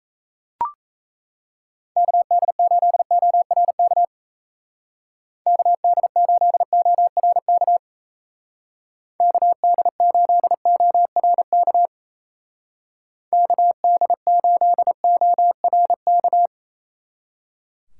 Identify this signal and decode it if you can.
Morse code, heard as a keyed tone, about 37 wpm, 700 Hz. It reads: KD8ORK KD8ORK KD8ORK KD8ORK